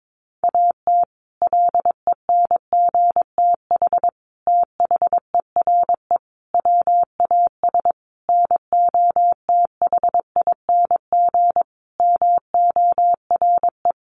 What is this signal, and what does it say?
Morse code, heard as a keyed tone, about 22 wpm, 700 Hz. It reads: AT LENGTH THERE WAS NOTHING MORE
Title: Sentences from Top 1000 Words 22wpm